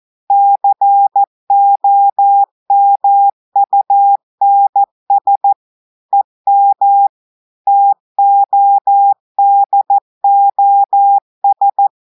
Morse code, heard as a keyed tone, about 14 wpm, 800 Hz. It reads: COMUNS EM TODOS
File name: audios / portugais / portugais texte en clair 2 vitesse 14 mots minutes